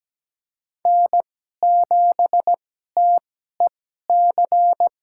Morse code, heard as a keyed tone, about 17 wpm, spaced as for 12 wpm, 700 Hz. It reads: N7TEC
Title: USA Call Signs - 5 Characters - 2x Character Spacing - 17wpm